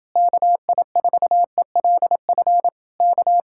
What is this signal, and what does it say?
Morse code, heard as a keyed tone, about 29 wpm, 700 Hz. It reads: KI4ELF K